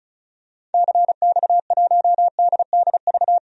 Morse code, heard as a keyed tone, about 36 wpm, 700 Hz. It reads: CX1DDV
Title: DXCC-100 Call Signs - 6 Characters - 35wpm